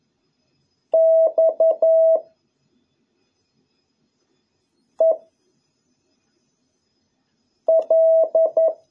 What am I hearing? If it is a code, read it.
Morse code, heard as a keyed tone, about 11 wpm, 650 Hz. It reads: X E L